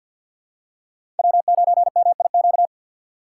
Morse code, heard as a keyed tone, about 48 wpm, 690 Hz. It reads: W9MIQ
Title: USA Call Signs - 5 Characters - 2x Character Spacing - 50wpm